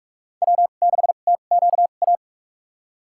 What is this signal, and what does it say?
Morse code, heard as a keyed tone, about 46 wpm, spaced as for 33 wpm, 700 Hz. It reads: W6TQA
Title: USA Call Signs - 5 Characters - 2x Character Spacing - 45wpm